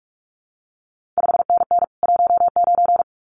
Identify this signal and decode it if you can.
Morse code, heard as a keyed tone, about 46 wpm, 700 Hz. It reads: 5NN 19
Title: CQ WW DX Contest Exchange Encore - 45wpm